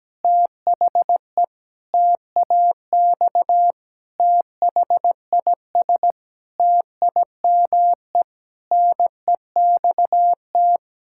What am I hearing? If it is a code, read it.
Morse code, heard as a keyed tone, about 17 wpm, 700 Hz. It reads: THE TAX THIS TIME NEXT
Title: Sentences from Top 1000 Words - Review Part 5 - 17wpm